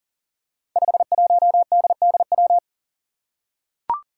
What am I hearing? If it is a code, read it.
Morse code, heard as a keyed tone, about 42 wpm, 700 Hz. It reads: 51DDW
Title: Code Groups - 5 Letter:Numbers - 40wpm